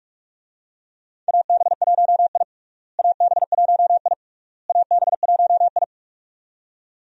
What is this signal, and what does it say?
Morse code, heard as a keyed tone, about 46 wpm, 700 Hz. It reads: AB1I AB1I AB1I
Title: USA Call Signs - 4 Characters - Repeated 3x - 45wpm